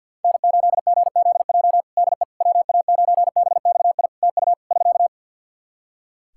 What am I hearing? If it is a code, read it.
Morse code, heard as a keyed tone, about 52 wpm, 700 Hz. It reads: N8GZJ DE WA9BXI TU <SK>
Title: K1USN SST QSO Practice 50wpm